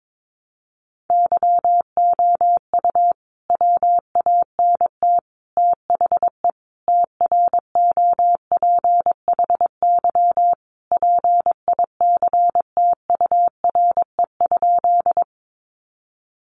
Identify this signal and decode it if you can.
Morse code, heard as a keyed tone, about 22 wpm, 700 Hz. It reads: YOU WANT THE TROPHY PICTURE?
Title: A Sound Of Thunder 22wpm